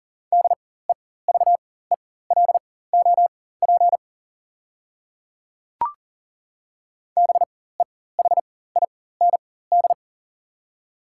Morse code, heard as a keyed tone, about 40 wpm, spaced as for 19 wpm, 690 Hz. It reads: DEVELOP BEHIND
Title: 4x Character Spacing - Top 500 Words - 40wpm